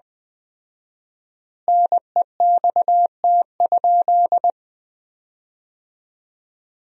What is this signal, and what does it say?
Morse code, heard as a keyed tone, about 20 wpm, 700 Hz. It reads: NEXT?
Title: 4x Word-Spacing - Sentences from Top 300 Words - 20wpm